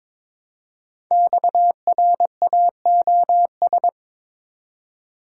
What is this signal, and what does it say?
Morse code, heard as a keyed tone, about 22 wpm, 700 Hz. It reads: XRAOS
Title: Code Groups - 5 Letters - 22wpm